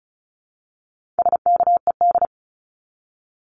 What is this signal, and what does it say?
Morse code, heard as a keyed tone, about 34 wpm, 700 Hz. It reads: SKED
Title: Abbreviations 35wpm